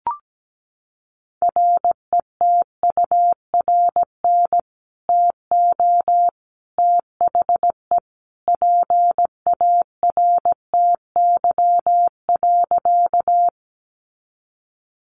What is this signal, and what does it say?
Morse code, heard as a keyed tone, about 17 wpm, 700 Hz. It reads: RETURN TO THE PARTY.
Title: Sentences from Top 1400 Words 17wpm